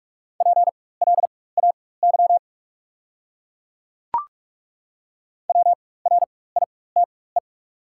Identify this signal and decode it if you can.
Morse code, heard as a keyed tone, about 42 wpm, spaced as for 21 wpm, 690 Hz. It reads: PLAY WRITE
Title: 4x Character Spacing - Top 300 Words - 45wpm